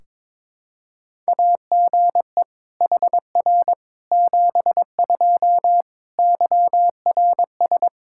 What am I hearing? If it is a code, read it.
Morse code, heard as a keyed tone, about 22 wpm, 700 Hz. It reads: AGE HR 72 YRS
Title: Practice QSO - Exchange 3 - More Info - 22wpm